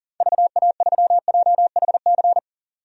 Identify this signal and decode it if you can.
Morse code, heard as a keyed tone, about 40 wpm, 700 Hz. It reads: VA3JHC
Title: DXCC-100 Call Signs - 6 Characters - 40wpm